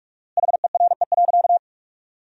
Morse code, heard as a keyed tone, about 46 wpm, 700 Hz. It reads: HERE.
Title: 4x Word-Spacing - Sentences from Top 500 Words - 45wpm